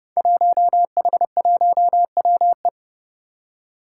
Morse code, heard as a keyed tone, about 30 wpm, 700 Hz. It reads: 1H1WE
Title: Five Challenging Letter-Number Combinations 30wpm